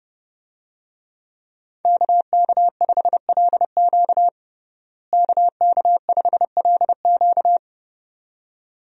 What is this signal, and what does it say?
Morse code, heard as a keyed tone, about 32 wpm, 700 Hz. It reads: KK5LQ KK5LQ
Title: USA Call Signs - 5 Characters - Repeated 2x - 30wpm